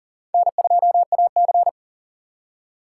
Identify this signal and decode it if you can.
Morse code, heard as a keyed tone, about 51 wpm, 700 Hz. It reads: N2AC